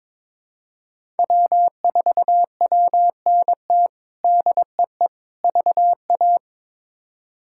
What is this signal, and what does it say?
Morse code, heard as a keyed tone, about 22 wpm, 700 Hz. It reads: W4WNT DEE VA
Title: K1USN SST QSO Practice - 22wpm